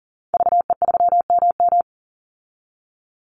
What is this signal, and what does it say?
Morse code, heard as a keyed tone, about 40 wpm, 700 Hz. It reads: VE3MM
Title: Call Signs - 40wpm